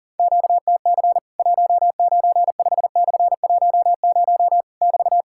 Morse code, heard as a keyed tone, about 37 wpm, 700 Hz. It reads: QTC 195/10 <BT>
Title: Worked All Europe DX Contest - QTC Exchange - 40wpm